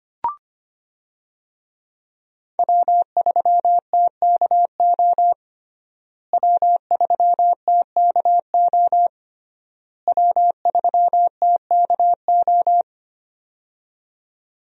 Morse code, heard as a keyed tone, about 25 wpm, 700 Hz. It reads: W3TKO W3TKO W3TKO
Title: USA Call Signs - 5 Characters - Repeated 3x - 25wpm